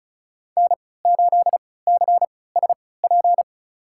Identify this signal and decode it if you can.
Morse code, heard as a keyed tone, about 36 wpm, spaced as for 20 wpm, 700 Hz. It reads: N8CSP
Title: USA Call Signs - 5 Characters - 3x Character Spacing - 35wpm